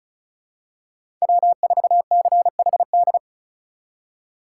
Morse code, heard as a keyed tone, about 34 wpm, 700 Hz. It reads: W4CHD
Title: USA Call Signs - 5 Characters - 4x Character Spacing - 35wpm